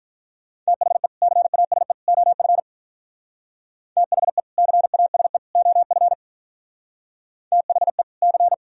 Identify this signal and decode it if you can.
Morse code, heard as a keyed tone, about 52 wpm, 700 Hz. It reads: THE CASE OF THE CASE OF THE C